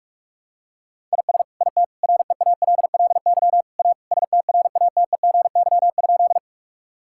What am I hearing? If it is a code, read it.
Morse code, heard as a keyed tone, about 46 wpm, 700 Hz. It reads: IS IT REALLY A STRATEGY?
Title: Sentences from Top 1000 Words - Review Part 5 - 45wpm